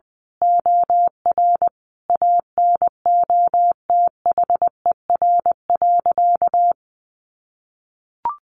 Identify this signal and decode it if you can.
Morse code, heard as a keyed tone, about 20 wpm, 700 Hz. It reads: OR ANOTHER.
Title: Sentences from Top 1400 Words 20wpm